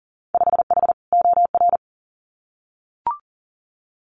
Morse code, heard as a keyed tone, about 40 wpm, 700 Hz. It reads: 5H OR
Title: Winter Field Day Exchanges Encore 40wpm